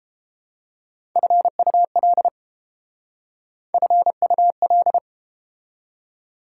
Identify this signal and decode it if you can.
Morse code, heard as a keyed tone, about 32 wpm, 700 Hz. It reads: FUL FUL